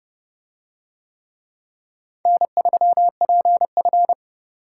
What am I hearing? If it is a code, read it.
Morse code, heard as a keyed tone, about 26 wpm, 700 Hz. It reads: N3PF